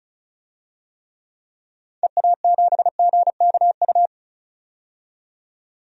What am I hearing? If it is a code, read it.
Morse code, heard as a keyed tone, about 34 wpm, 700 Hz. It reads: EA7GKU
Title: DXCC-100 Call Signs - 6 Characters - 35wpm